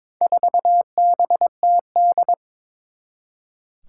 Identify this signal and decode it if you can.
Morse code, heard as a keyed tone, about 22 wpm, 700 Hz. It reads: 4BTD
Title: Four Challenging Letter-Number Combinations 22wpm